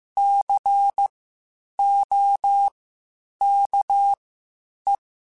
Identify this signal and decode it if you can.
Morse code, heard as a keyed tone, about 15 wpm, spaced as for 8 wpm, 800 Hz. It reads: COKE
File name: Spaced Alphas / Spaced Alphas mp3s / Cs sp @13